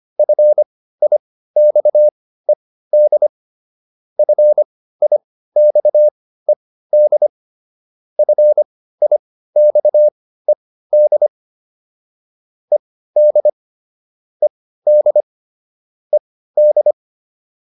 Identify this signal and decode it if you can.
Morse code, heard as a keyed tone, about 25 wpm, spaced as for 15 wpm, 600 Hz. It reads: FIXED FIXED FIXED ED ED ED